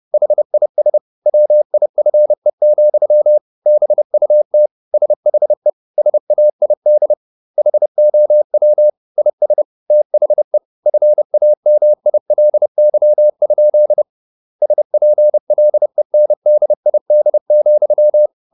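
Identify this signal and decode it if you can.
Morse code, heard as a keyed tone, about 30 wpm, 600 Hz. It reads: HIS WIFE, BUT SHE SAID HOW IS THE FAMILY? SPLENDID,
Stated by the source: Morse code at 30 wpm, the novel Never by Ken Follett in English, from CW med Gnister.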